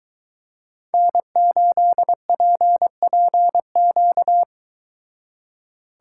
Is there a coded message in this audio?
Yes: Morse code from 0.9 to 4.4 s, heard as a keyed tone, about 23 wpm, 700 Hz. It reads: N8PPQ